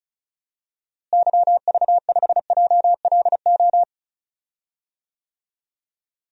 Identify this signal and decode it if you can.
Morse code, heard as a keyed tone, about 36 wpm, 700 Hz. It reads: YV5JLO
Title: DXCC-100 Call Signs - 6 Characters - Repeated 3x - 35wpm